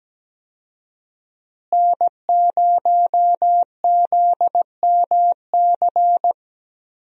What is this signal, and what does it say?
Morse code, heard as a keyed tone, about 17 wpm, 700 Hz. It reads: N0ZMC
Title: USA Call Signs - 5 Characters - Repeated 3x - 17wpm